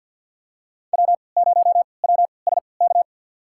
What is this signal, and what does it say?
Morse code, heard as a keyed tone, about 51 wpm, spaced as for 28 wpm, 700 Hz. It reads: W0WSK
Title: USA Call Signs - 5 Characters - 3x Character Spacing - 50wpm